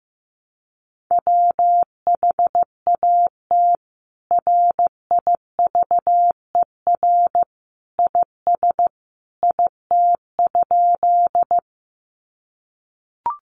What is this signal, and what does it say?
Morse code, heard as a keyed tone, about 15 wpm, 700 Hz. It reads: WHAT RIVER IS IT?
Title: Sentences from Top 300 Words 15wpm